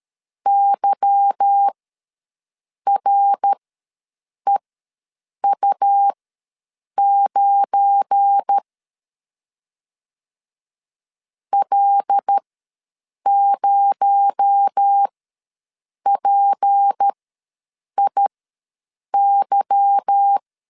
Morse code, heard as a keyed tone, about 13 wpm, spaced as for 7 wpm, 800 Hz. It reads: YREU9 L0PIY
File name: LEZIONI / 8 YJ45 / 8 Y ws 12wpm 3409